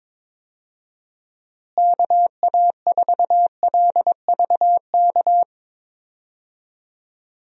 Morse code, heard as a keyed tone, about 21 wpm, 700 Hz. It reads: KA4LVK
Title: USA Call Signs - 6 Characters - Repeated 3x - 22wpm